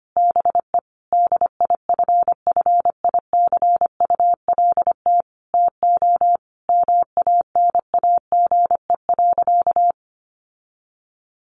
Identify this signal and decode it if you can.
Morse code, heard as a keyed tone, about 25 wpm, 700 Hz. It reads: BE DIFFICULT TO MANAGE.
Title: Sentences from Top 1400 Words 25wpm